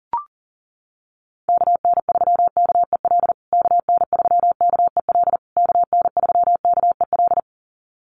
Morse code, heard as a keyed tone, about 40 wpm, 700 Hz. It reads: KN3KEL KN3KEL KN3KEL
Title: USA Call Signs Repeated 3x 40wpm